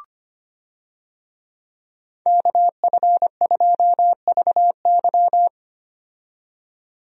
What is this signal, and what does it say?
Morse code, heard as a keyed tone, about 25 wpm, 700 Hz. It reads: KF2VY
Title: USA Call Signs - 5 Characters - 25wpm